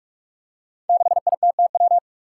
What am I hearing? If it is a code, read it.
Morse code, heard as a keyed tone, about 46 wpm, 700 Hz. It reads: 6ITTW
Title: Five Challenging Letter-Number Combinations 45wpm